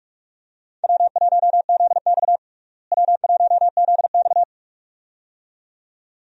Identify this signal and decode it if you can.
Morse code, heard as a keyed tone, about 45 wpm, 700 Hz. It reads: W1ZX W1ZX
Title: USA Call Signs - 4 Characters - Repeated 2x - 45wpm